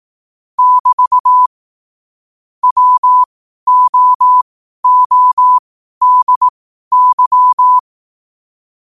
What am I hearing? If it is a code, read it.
Morse code, heard as a keyed tone, about 18 wpm, spaced as for 12 wpm, 1000 Hz. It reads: <BT> WOODY
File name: qotdfiles / 12wpm